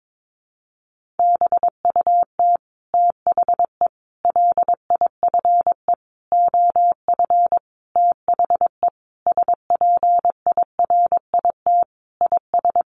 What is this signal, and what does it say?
Morse code, heard as a keyed tone, about 22 wpm, 700 Hz. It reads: BUT THE LIFE OF THE SPIRIT IS
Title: Sentences from Top 1500 Words 22wpm